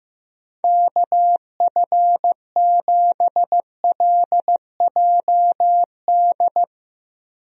Morse code, heard as a keyed tone, about 16 wpm, 700 Hz. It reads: KF7LJD
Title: USA Call Signs - 6 Characters - Repeated 2x - 15wpm